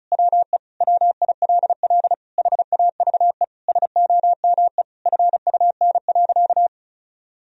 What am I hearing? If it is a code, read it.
Morse code, heard as a keyed tone, about 34 wpm, 700 Hz. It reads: WE WILL HAVE SOME FUN.
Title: Sentences from Top 1000 Words - Review Part 4 - 35wpm